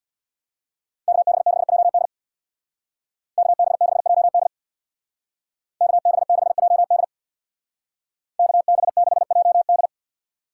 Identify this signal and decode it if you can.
Morse code, heard as a keyed tone, about 60 wpm, 700 Hz. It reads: KB6JD KB6JD KB6JD KB6JD
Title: Speed Racing - USA 2x2 Call Signs 50wpm